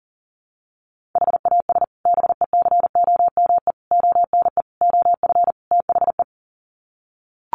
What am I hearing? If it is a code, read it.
Morse code, heard as a keyed tone, about 40 wpm, 700 Hz. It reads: HAS BECOME ONE OF THE
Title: Sets of 5 Words Encore 40wpm